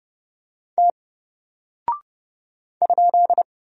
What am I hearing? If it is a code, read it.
Morse code, heard as a keyed tone, about 30 wpm, 700 Hz. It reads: T ?
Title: Single Character (Letter:Number:Punctuation:Prosign) Alt 30wpm